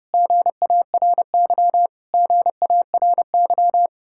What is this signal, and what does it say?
Morse code, heard as a keyed tone, about 33 wpm, 700 Hz. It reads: GARY GARY